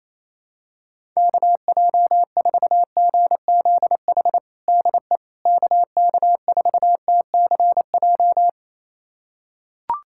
Morse code, heard as a keyed tone, about 27 wpm, 700 Hz. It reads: KJ4GZH DE KK4TCJ
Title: USA Callsign DE Callsign - 6 Characters - 28wpm